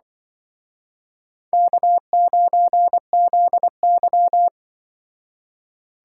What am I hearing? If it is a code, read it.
Morse code, heard as a keyed tone, about 24 wpm, 700 Hz. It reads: K9ZY
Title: Speed Racing - USA 1x2 Call Signs 20wpm